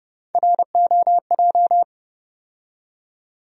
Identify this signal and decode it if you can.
Morse code, heard as a keyed tone, about 30 wpm, 700 Hz. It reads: ROJ
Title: USA Call Sign Postfixes - 3 Letters 30wpm